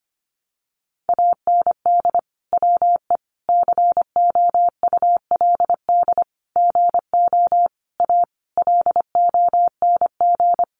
Morse code, heard as a keyed tone, about 25 wpm, 700 Hz. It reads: AND WE COULD GO A LONG
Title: Sentences from Top 1200 Words 25wpm